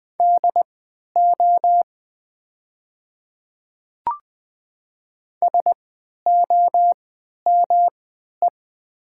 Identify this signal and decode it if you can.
Morse code, heard as a keyed tone, about 21 wpm, spaced as for 12 wpm, 700 Hz. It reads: DO SOME
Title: 3x Character Spacing - Top 200 Words - 20wpm